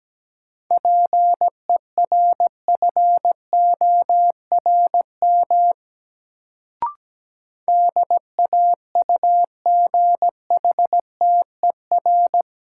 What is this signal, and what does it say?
Morse code, heard as a keyed tone, about 17 wpm, 700 Hz. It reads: PERFORM DAUGHTER
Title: Sentences from Top 1500 Words - New Word Drill Alt - 17wpm